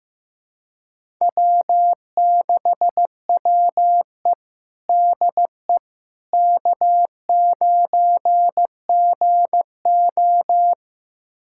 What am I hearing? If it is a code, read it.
Morse code, heard as a keyed tone, about 15 wpm, 700 Hz. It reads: W6WE DE K9GO
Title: USA Callsign DE Callsign - 4 Characters - 15wpm